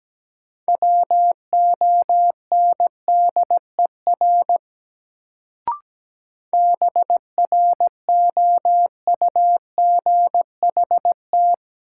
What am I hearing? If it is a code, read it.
Morse code, heard as a keyed tone, about 17 wpm, 700 Hz. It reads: WONDER BROUGHT
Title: Sentences from Top 500 Words - New Word Drill - 17wpm